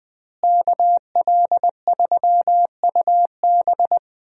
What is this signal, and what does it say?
Morse code, heard as a keyed tone, about 21 wpm, 700 Hz. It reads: KL3UB
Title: USA Call Signs - 5 Characters - Repeated 2x - 20wpm